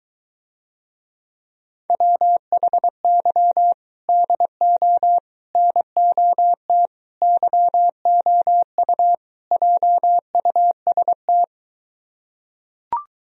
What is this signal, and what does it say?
Morse code, heard as a keyed tone, about 23 wpm, 700 Hz. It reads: WHY DO NOT YOU JUST